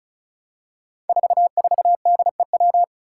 Morse code, heard as a keyed tone, about 34 wpm, 700 Hz. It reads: 44DEW
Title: Five Challenging Letter-Number Combinations 35wpm